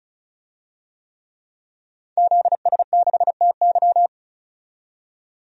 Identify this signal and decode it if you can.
Morse code, heard as a keyed tone, about 36 wpm, 700 Hz. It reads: ZS6TY
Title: DXCC-100 Call Signs - 5 Characters - 35wpm